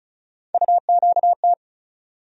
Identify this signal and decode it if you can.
Morse code, heard as a keyed tone, about 36 wpm, 700 Hz. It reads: UQT
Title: USA Call Sign Postfixes - 3 Letters 35wpm